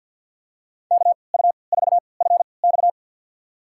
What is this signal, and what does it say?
Morse code, heard as a keyed tone, about 52 wpm, spaced as for 29 wpm, 700 Hz. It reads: KU4FX
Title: USA Call Signs - 5 Characters - 3x Character Spacing - 50wpm